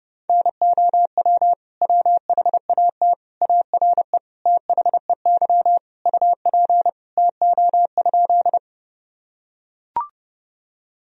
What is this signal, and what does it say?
Morse code, heard as a keyed tone, about 30 wpm, 700 Hz. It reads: NOW WHAT ARE THEY UP TO?
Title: Sentences from Top 1000 Words - Review Part 1 - 30wpm